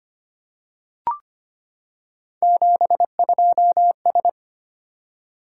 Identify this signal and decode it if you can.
Morse code, heard as a keyed tone, about 25 wpm, 700 Hz. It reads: 72S
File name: Three Letter-Number Combinations - 25wpm